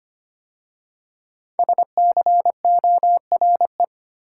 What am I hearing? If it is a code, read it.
Morse code, heard as a keyed tone, about 25 wpm, 700 Hz. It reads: SCORE
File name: 5 Letter Words - First 3 Letters Given - 25wpm